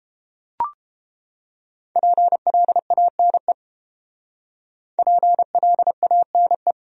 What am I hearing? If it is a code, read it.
Morse code, heard as a keyed tone, about 31 wpm, 700 Hz. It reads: PLANE PLANE